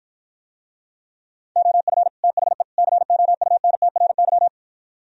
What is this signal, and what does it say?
Morse code, heard as a keyed tone, about 52 wpm, 700 Hz. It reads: OF THE COUNTRY